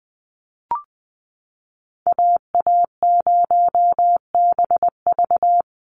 Morse code, heard as a keyed tone, about 20 wpm, 700 Hz. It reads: AA0BV
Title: NA SOTA Activator Callsigns 20wpm